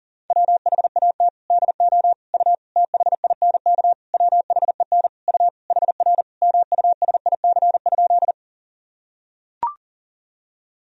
Morse code, heard as a keyed tone, about 40 wpm, 700 Hz. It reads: WHAT DO U THINK WHEN U HR MUSIC?
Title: Sentences from Top 300 Words and Abbreviations 40wpm